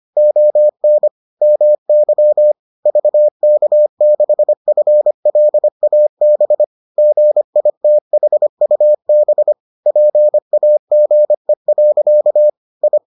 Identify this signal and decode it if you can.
Morse code, heard as a keyed tone, about 25 wpm, 600 Hz. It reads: ON MY VK6FLAB GITHUB PAGE. I